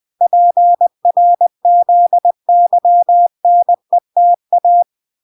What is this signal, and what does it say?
Morse code, heard as a keyed tone, about 20 wpm, 700 Hz. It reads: PRZYNETA